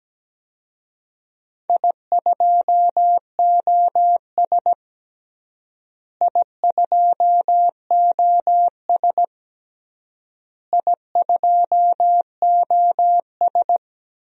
Morse code, heard as a keyed tone, about 17 wpm, 700 Hz. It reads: I2OS I2OS I2OS
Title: DXCC-100 Call Signs - 4 Characters - Repeated 3x - 17wpm